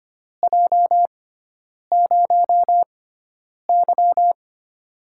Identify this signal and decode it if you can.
Morse code, heard as a keyed tone, about 25 wpm, spaced as for 9 wpm, 700 Hz. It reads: J0Y
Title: USA Call Signs - 6x Character Spacing - 25wpm